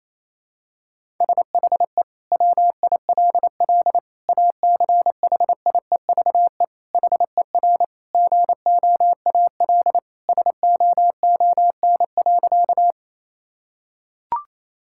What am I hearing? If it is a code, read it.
Morse code, heard as a keyed tone, about 28 wpm, 700 Hz. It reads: SHE WILL ACHIEVE HER GOAL SOON.
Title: Sentences from Top 1000 Words - Review Part 6 - 28wpm